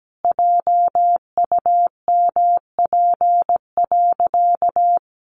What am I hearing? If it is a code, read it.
Morse code, heard as a keyed tone, about 17 wpm, 700 Hz. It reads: JUMP.
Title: Sentences from Top 1300 Words 17wpm